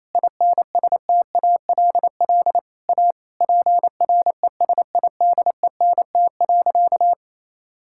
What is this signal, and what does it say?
Morse code, heard as a keyed tone, about 28 wpm, 700 Hz. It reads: INSTALL A PRESIDENT.
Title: Sentences from Top 1600 Words 28wpm